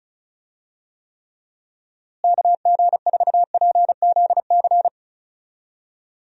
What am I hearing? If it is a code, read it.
Morse code, heard as a keyed tone, about 34 wpm, 700 Hz. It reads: KG4PZC
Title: USA Call Signs - 6 Characters - Repeated 3x - 35wpm